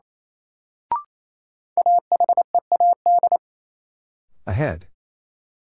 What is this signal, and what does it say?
Morse code, heard as a keyed tone, about 28 wpm, 700 Hz. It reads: AHEAD
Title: Sentences from Top 1100 Words - New Word Drill Alt - 28wpm